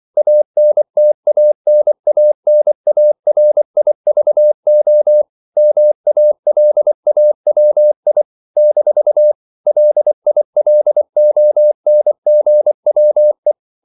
Morse code, heard as a keyed tone, about 24 wpm, 600 Hz. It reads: ANTANANARIVO MALAWI - LILONGWE
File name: CW / Practice / NationalCapitals24